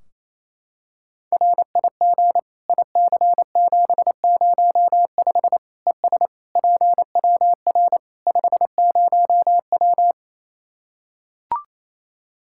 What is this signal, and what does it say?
Morse code, heard as a keyed tone, about 28 wpm, 700 Hz. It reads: RIG IC705 ES PWR 50W
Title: Practice QSO - Exchange 2 - Info - 28wpm